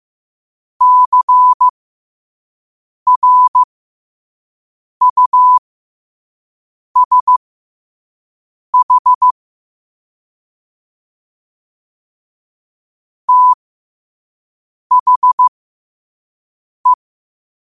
Morse code, heard as a keyed tone, about 15 wpm, spaced as for 5 wpm, 1000 Hz. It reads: CRUSH THE
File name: qotdfiles / 5wpm